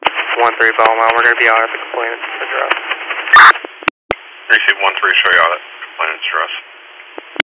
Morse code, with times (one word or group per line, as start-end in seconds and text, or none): none